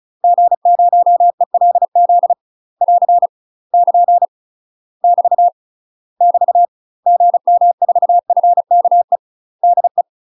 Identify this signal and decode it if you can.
Morse code, heard as a keyed tone, about 34 wpm, 700 Hz. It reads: G0ELZ <AR> <KN> <BT> <BT> GM4FKE DE
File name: UK MASTER QSOs TRAINING / 35WPM / UK MASTER QSOs 0017